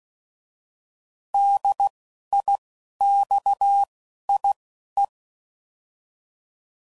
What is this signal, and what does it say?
Morse code, heard as a keyed tone, about 16 wpm, spaced as for 11 wpm, 800 Hz. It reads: DIXIE
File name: Prefixes / By First Letter / Ds